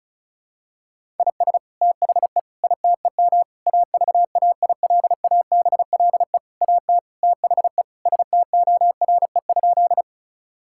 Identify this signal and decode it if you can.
Morse code, heard as a keyed tone, about 36 wpm, 700 Hz. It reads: IS THE ITEM AVAILABLE AT THE STORE?
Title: Sentences from Top 1000 Words - Review Part 6 - 35wpm